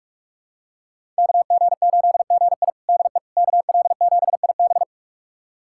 Morse code, heard as a keyed tone, about 44 wpm, 700 Hz. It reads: KG8GI DE KL7IB
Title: USA Callsign DE Callsign 45wpm